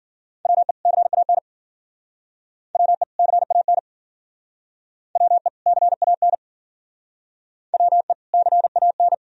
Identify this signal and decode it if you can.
Morse code, heard as a keyed tone, about 48 wpm, 700 Hz. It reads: WE CAN WE CAN WE CAN WE CAN
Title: Speed-Racing - Sets of 2 Words 40wpm